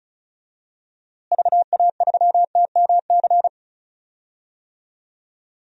Morse code, heard as a keyed tone, about 34 wpm, 700 Hz. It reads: VA3TMC
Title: DXCC-100 Call Signs - 6 Characters - Repeated 2x - 35wpm